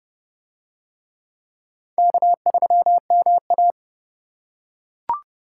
Morse code, heard as a keyed tone, about 28 wpm, 700 Hz. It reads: K3MA